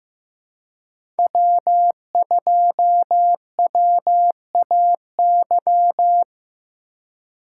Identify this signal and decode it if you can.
Morse code, heard as a keyed tone, about 16 wpm, 700 Hz. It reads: W2WAY
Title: USA Call Signs - 5 Characters - 4x Character Spacing - 15wpm